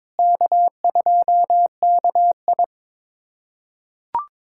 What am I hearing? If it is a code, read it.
Morse code, heard as a keyed tone, about 22 wpm, 700 Hz. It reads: K2KI
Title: USA Callsign DE Callsign - 4 Characters - 22wpm